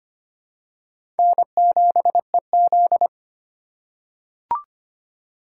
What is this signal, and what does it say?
Morse code, heard as a keyed tone, about 25 wpm, 700 Hz. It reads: N7EZ
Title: USA Call Signs - 4 Characters - 2x Character Spacing - 25wpm